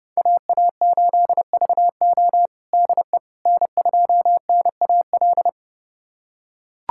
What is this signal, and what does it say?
Morse code, heard as a keyed tone, about 30 wpm, 700 Hz. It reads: AA8VO DE N2NAL
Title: USA Callsign DE Callsign - 5 Characters - 30wpm